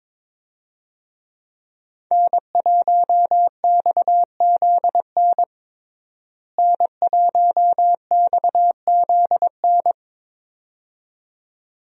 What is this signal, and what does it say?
Morse code, heard as a keyed tone, about 23 wpm, 700 Hz. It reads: N1XZN N1XZN
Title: USA Call Signs - 5 Characters - Repeated 2x - 22wpm